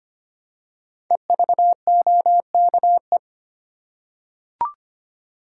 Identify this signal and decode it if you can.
Morse code, heard as a keyed tone, about 25 wpm, 700 Hz. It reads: EVOKE
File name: 5 Letter Words - First 3 Letters Given - 25wpm